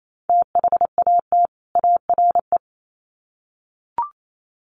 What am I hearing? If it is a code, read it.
Morse code, heard as a keyed tone, about 28 wpm, 700 Hz. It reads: THAT ARE
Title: Sets of 2 Words 28wpm